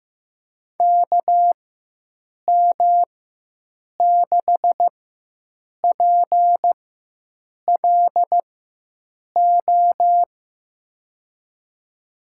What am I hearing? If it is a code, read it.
Morse code, heard as a keyed tone, about 15 wpm, spaced as for 7 wpm, 700 Hz. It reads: KM6PLO
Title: USA Call Signs - 4x Character Spacing - 15wpm